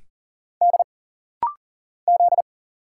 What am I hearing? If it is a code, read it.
Morse code, heard as a keyed tone, about 40 wpm, 700 Hz. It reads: D Z
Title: Single Letters - Rapid-Fire 40wpm